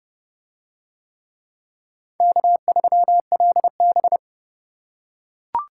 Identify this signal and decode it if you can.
Morse code, heard as a keyed tone, about 28 wpm, 700 Hz. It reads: K3LB